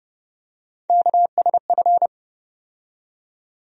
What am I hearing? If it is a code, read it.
Morse code, heard as a keyed tone, about 32 wpm, 700 Hz. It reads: KSF
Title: USA Call Sign Postfixes - 3 Letters 30wpm